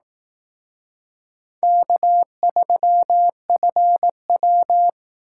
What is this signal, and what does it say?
Morse code, heard as a keyed tone, about 18 wpm, 700 Hz. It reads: K3FW
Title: Speed Racing - USA Call Signs 15wpm